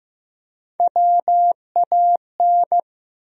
Morse code, heard as a keyed tone, about 15 wpm, 700 Hz. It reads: WAN
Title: USA Call Sign Postfixes - 3 Letters 15wpm